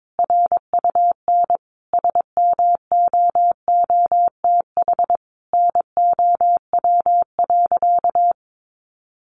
Morse code, heard as a keyed tone, about 22 wpm, 700 Hz. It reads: RUN SMOOTH NOW.
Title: Sentences from Top 1400 Words 22wpm